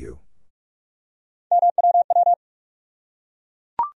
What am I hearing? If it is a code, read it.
Morse code, heard as a keyed tone, about 47 wpm, 700 Hz. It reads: MWW
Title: USA Call Sign Postfixes - 3 Letters 45wpm